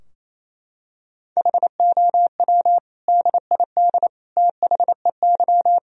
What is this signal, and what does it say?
Morse code, heard as a keyed tone, about 27 wpm, 700 Hz. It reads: HOW DID THEY
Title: Sentences from Top 1000 Words - Review Part 3 - 28wpm